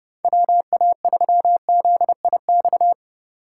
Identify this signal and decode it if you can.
Morse code, heard as a keyed tone, about 29 wpm, 700 Hz. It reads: WA3ZIX
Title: Speed Racing - USA Call Signs 30wpm